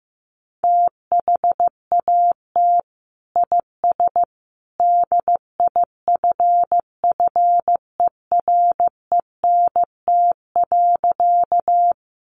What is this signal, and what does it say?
Morse code, heard as a keyed tone, about 15 wpm, 700 Hz. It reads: THAT IS DIFFERENT.
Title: Sentences from Top 600 Words 15wpm